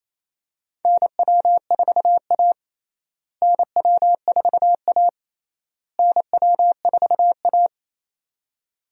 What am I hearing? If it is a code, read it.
Morse code, heard as a keyed tone, about 27 wpm, 700 Hz. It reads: NW4A NW4A NW4A
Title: USA Call Signs - 4 Characters - Repeated 3x - 28wpm